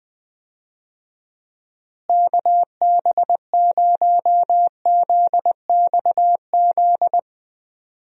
Morse code, heard as a keyed tone, about 20 wpm, 700 Hz. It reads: KB0ZXZ
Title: USA Call Signs - 6 Characters - Repeated 3x - 20wpm